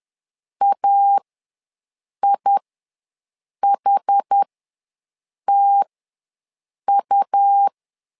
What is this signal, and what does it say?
Morse code, heard as a keyed tone, about 10 wpm, spaced as for 6 wpm, 800 Hz. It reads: AIHTU
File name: LEZIONI / 2 UHDN / 2 UH ws 10wpm 2029